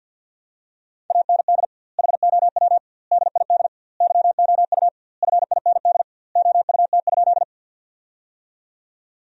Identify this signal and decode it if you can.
Morse code, heard as a keyed tone, about 50 wpm, 700 Hz. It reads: AND HOW DID YOU FIND OUT?
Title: 2x Word-Spacing - Sentences from Top 100 Words - 50wpm